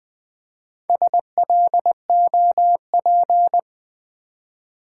Morse code, heard as a keyed tone, about 20 wpm, 700 Hz. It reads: SLOP
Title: Build A Word - 1 Letter At A Time - 5-Letter Words - 20wpm